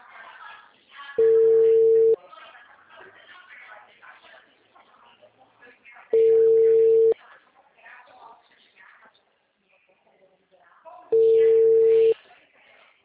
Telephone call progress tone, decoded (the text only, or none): ringback tone